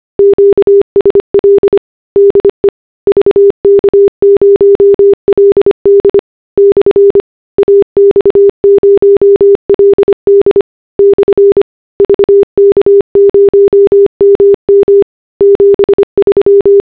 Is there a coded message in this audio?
Yes: Morse code from 0.2 to 16.8 s, heard as a keyed tone, about 25 wpm, 400 Hz. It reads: QSL DE VK0LD/AX0LD/VK0MM 73